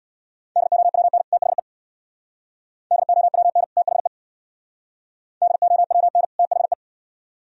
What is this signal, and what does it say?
Morse code, heard as a keyed tone, about 60 wpm, 700 Hz. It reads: DOWN THE DOWN THE DOWN THE
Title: Speed-Racing - Sets of 2 Words 50wpm